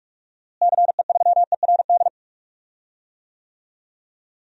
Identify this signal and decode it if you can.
Morse code, heard as a keyed tone, about 43 wpm, 700 Hz. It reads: CE3ERD
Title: DXCC-100 Call Signs - 6 Characters - Repeated 2x - 45wpm